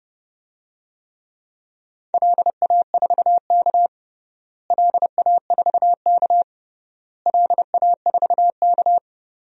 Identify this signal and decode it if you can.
Morse code, heard as a keyed tone, about 30 wpm, 700 Hz. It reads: LA4K LA4K LA4K
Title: DXCC-100 Call Signs - 4 Characters - Repeated 3x - 30wpm